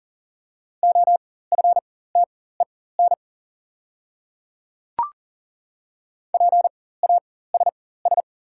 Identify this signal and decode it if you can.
Morse code, heard as a keyed tone, about 40 wpm, spaced as for 19 wpm, 690 Hz. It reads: OFTEN PASS
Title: 4x Character Spacing - Top 500 Words - 40wpm